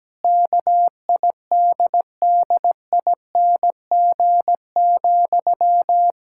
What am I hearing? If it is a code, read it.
Morse code, heard as a keyed tone, about 17 wpm, 700 Hz. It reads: KIDDING,
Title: Pangram Challenge v2 - 17wpm